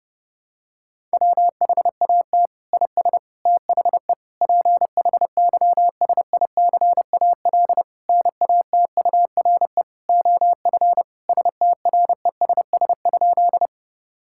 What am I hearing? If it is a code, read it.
Morse code, heard as a keyed tone, about 32 wpm, 700 Hz. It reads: WHAT IS THE PHYSICAL NATURE OF STRESS?
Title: Sentences from Top 1000 Words - Review Part 6 - 30wpm